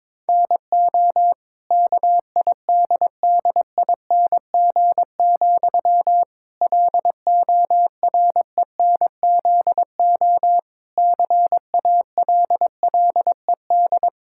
Morse code, heard as a keyed tone, about 22 wpm, 700 Hz. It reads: NO KIDDING, LORENZO CALLED
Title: Pangram Challenge v2 - 22wpm